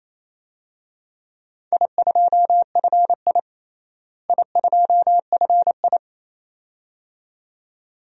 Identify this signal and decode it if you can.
Morse code, heard as a keyed tone, about 28 wpm, 700 Hz. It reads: I2FI I2FI
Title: DXCC-100 Call Signs - 4 Characters - Repeated 2x - 28wpm